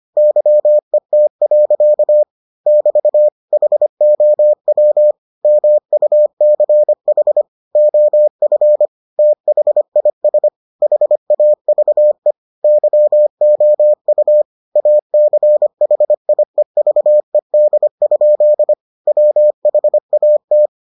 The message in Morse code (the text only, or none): YET. <BT> HOW MUCH OF THIS HAVE YOU ACHIEVED? WHAT